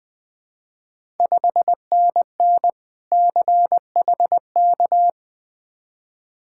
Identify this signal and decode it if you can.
Morse code, heard as a keyed tone, about 20 wpm, 700 Hz. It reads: 5NN CHK
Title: Ontario QSO Party Exchange 20wpm